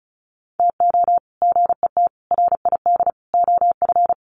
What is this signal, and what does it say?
Morse code, heard as a keyed tone, about 34 wpm, 700 Hz. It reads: TO GET RID OF
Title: Sets of 4 Words 35wpm